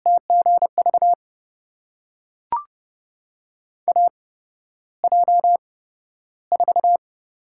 Morse code, heard as a keyed tone, about 30 wpm, spaced as for 13 wpm, 700 Hz. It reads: TGV AJ4
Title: USA Call Signs - 5 Characters - 8x Character Spacing - 30wpm